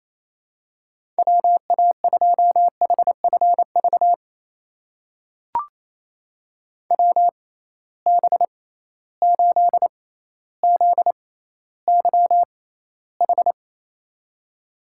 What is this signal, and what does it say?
Morse code, heard as a keyed tone, about 28 wpm, spaced as for 14 wpm, 700 Hz. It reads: WA2HFV WB8ZYH
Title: USA Call Signs - 6 Characters - 6x Character Spacing - 28wpm